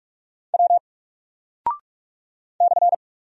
Morse code, heard as a keyed tone, about 44 wpm, 700 Hz. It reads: W /
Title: Single Character (Letter:Number:Punctuation:Prosign) 45wpm